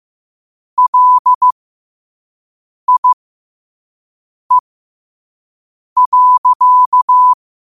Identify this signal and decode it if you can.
Morse code, heard as a keyed tone, about 15 wpm, spaced as for 5 wpm, 1000 Hz. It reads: LIE.
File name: qotdfiles / 5wpm